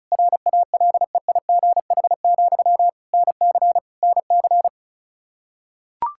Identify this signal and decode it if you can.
Morse code, heard as a keyed tone, about 36 wpm, 700 Hz. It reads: RALEIGH, NC NC
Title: K1USN SST QSO Practice 35wpm